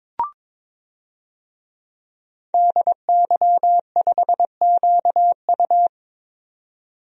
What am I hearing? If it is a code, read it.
Morse code, heard as a keyed tone, about 22 wpm, 700 Hz. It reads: DY5QU
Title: DXCC-100 Call Signs - 5 Characters - 22wpm